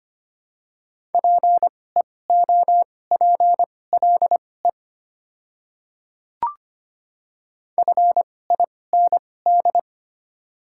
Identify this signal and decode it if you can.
Morse code, heard as a keyed tone, about 25 wpm, spaced as for 18 wpm, 690 Hz. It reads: PEOPLE FIND
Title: 2x Character Spacing - Top 100 Words - 25wpm